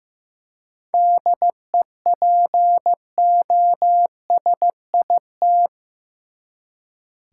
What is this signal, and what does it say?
Morse code, heard as a keyed tone, about 16 wpm, 700 Hz. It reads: DEPOSIT
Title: Seven Letter Words - 15wpm